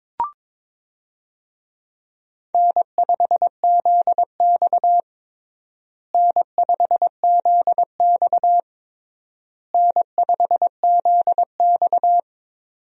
Morse code, heard as a keyed tone, about 22 wpm, 700 Hz. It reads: N5ZX N5ZX N5ZX
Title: USA Call Signs - 4 Characters - Repeated 3x - 22wpm